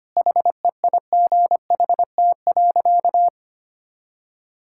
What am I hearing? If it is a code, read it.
Morse code, heard as a keyed tone, about 25 wpm, 700 Hz. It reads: HEIGHT.
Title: Sentences from Top 1600 Words 25wpm